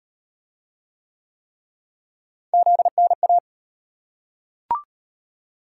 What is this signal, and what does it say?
Morse code, heard as a keyed tone, about 36 wpm, 700 Hz. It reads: ZNA